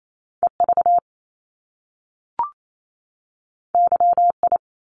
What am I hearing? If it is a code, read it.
Morse code, heard as a keyed tone, about 28 wpm, 700 Hz. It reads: EV YI
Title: Top Two Letter Combinations in Words 28wpm